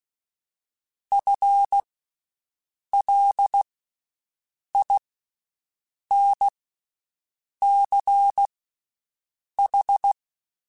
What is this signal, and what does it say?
Morse code, heard as a keyed tone, about 16 wpm, spaced as for 6 wpm, 800 Hz. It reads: FLINCH
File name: Special INGs-mp3s / Special INGs-mp3s / Slow ing @15